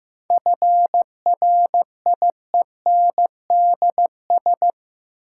Morse code, heard as a keyed tone, about 15 wpm, 700 Hz. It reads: FRIENDS
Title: Sentences from Top 1000 Words - Review Part 5 - New Word Drill - 15wpm